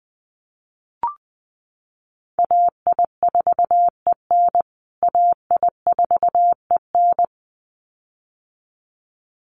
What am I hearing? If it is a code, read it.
Morse code, heard as a keyed tone, about 20 wpm, 700 Hz. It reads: AI4EN AI4EN
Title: USA Call Signs Repeated 2x 20wpm